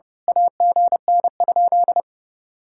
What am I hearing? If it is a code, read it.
Morse code, heard as a keyed tone, about 33 wpm, 700 Hz. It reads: AGN?
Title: Speed-Racing Common QSO Phrases Encore - 30wpm